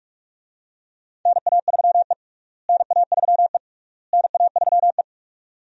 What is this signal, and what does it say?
Morse code, heard as a keyed tone, about 45 wpm, 700 Hz. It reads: NA3E NA3E NA3E
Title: USA Call Signs - 4 Characters - Repeated 3x - 45wpm